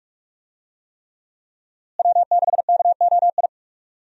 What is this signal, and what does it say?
Morse code, heard as a keyed tone, about 45 wpm, 700 Hz. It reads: W6KOI
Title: USA Call Signs - 5 Characters - Repeated 3x - 45wpm